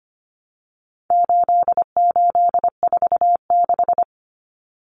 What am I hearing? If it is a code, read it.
Morse code, heard as a keyed tone, about 25 wpm, 700 Hz. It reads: 8846
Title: Numbers - 4-Digit 25wpm